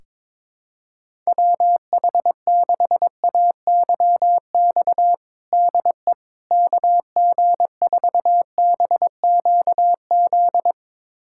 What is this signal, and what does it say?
Morse code, heard as a keyed tone, about 21 wpm, 700 Hz. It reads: WH6AYX DE KG4BQZ
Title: USA Callsign DE Callsign 22wpm